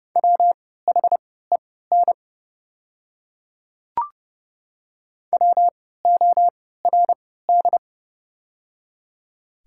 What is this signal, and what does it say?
Morse code, heard as a keyed tone, about 30 wpm, spaced as for 17 wpm, 700 Hz. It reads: WHEN WORD
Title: 3x Character Spacing - Top 100 Words - 30wpm